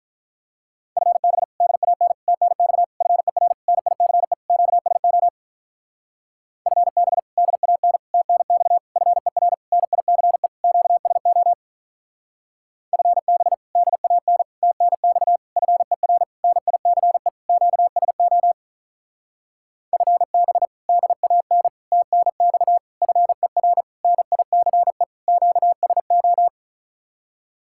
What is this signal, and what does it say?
Morse code, heard as a keyed tone, about 43 wpm, 700 Hz. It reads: FB DAN TNX FER NICE QSO FB DAN TNX FER NICE QSO FB DAN TNX FER NICE QSO FB DAN TNX FER NICE QSO
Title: Speed-Racing Common QSO Phrases - 35wpm